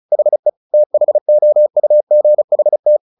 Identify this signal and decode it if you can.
Morse code, heard as a keyed tone, about 34 wpm, 600 Hz. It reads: HE THOUGHT
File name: the old man and the sea 2 35wpm